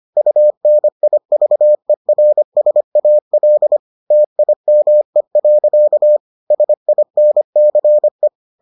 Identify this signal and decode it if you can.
Morse code, heard as a keyed tone, about 25 wpm, 600 Hz. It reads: UNIVERSAL TIME. SINCE